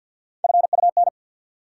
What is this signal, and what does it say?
Morse code, heard as a keyed tone, about 51 wpm, 700 Hz. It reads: FUN
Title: Sentences from Top 700 Words - New Word Drill - 50wpm